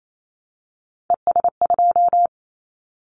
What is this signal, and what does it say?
Morse code, heard as a keyed tone, about 28 wpm, 700 Hz. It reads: ES2
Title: Callsign Prefixes for Top 100 DXCC - 28wpm